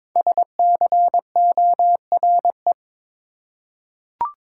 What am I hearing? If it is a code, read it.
Morse code, heard as a keyed tone, about 22 wpm, 700 Hz. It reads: SCORE
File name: 5 Letter Words - First 3 Letters Given - 22wpm